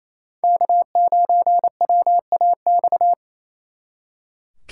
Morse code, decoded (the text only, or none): K9WAX